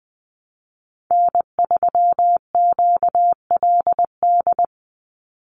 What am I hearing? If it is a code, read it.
Morse code, heard as a keyed tone, about 20 wpm, 700 Hz. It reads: N3QLD
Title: USA Call Signs 20wpm